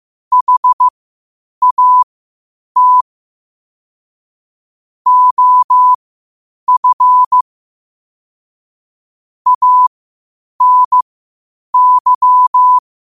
Morse code, heard as a keyed tone, about 15 wpm, spaced as for 8 wpm, 1000 Hz. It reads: HAT OF ANY
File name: qotdfiles / 8wpm